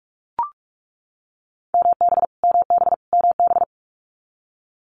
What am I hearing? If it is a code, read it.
Morse code, heard as a keyed tone, about 44 wpm, 700 Hz. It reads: MB MB MB
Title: Canadian Provinces and Territories - Repeated 3x - 45wpm